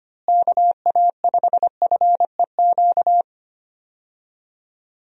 Morse code, heard as a keyed tone, about 25 wpm, 700 Hz. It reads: KA5FEQ
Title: USA Call Signs - 6 Characters - Repeated 3x - 25wpm